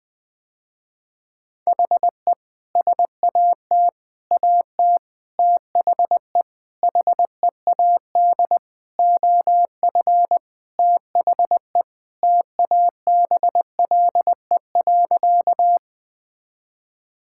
Morse code, heard as a keyed tone, about 20 wpm, 700 Hz. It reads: HE SAT AT THE HEAD OF THE TABLE.
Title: Sentences with 5-Word Phrases - Spoken Beginning - 20wpm